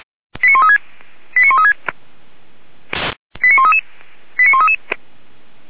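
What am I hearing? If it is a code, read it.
Five-tone selective call (ZVEI), heard as quick runs of five tones, tones of about 70 ms. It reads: 89136 89136 89133 89133
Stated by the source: ZVEI five-tone call received with an SDR for the Feuerwehr Orlamünde fire brigade.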